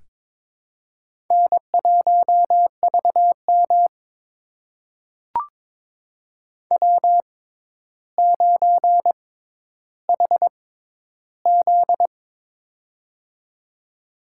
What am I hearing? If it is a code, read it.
Morse code, heard as a keyed tone, about 22 wpm, spaced as for 11 wpm, 700 Hz. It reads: N1VM W9HZ
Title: USA Call Signs - 4 Characters - 6x Character Spacing - 22wpm